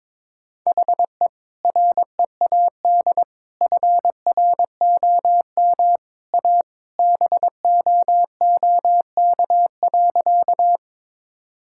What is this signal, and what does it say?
Morse code, heard as a keyed tone, about 21 wpm, 700 Hz. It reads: HE READ FROM A BOOK.
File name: practice / 22wpm / Sentences with 2-Word Phrases - 22wpm